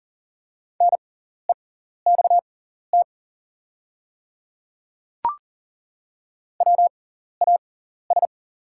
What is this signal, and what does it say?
Morse code, heard as a keyed tone, about 40 wpm, spaced as for 14 wpm, 690 Hz. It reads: NEXT WAS
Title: 6x Character Spacing - Top 300 Words - 40wpm